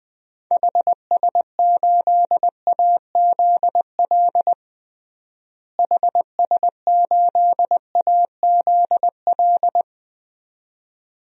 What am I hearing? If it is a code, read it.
Morse code, heard as a keyed tone, about 20 wpm, 700 Hz. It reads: HS8AZL HS8AZL
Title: DXCC-100 Call Signs - 6 Characters - Repeated 2x - 20wpm